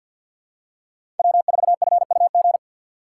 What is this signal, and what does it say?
Morse code, heard as a keyed tone, about 52 wpm, 700 Hz. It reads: W4FUG
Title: USA Call Signs - 2x Character Spacing - 50wpm